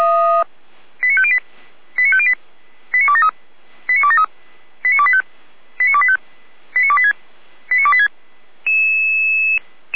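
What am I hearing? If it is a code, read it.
Five-tone selective call (ZVEI), heard as quick runs of five tones, tones of about 70 ms. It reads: 89508 89508 89273 89273 89285 89285 89286 89286
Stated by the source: Five-tone ZVEI selective call 89208, received by SDR.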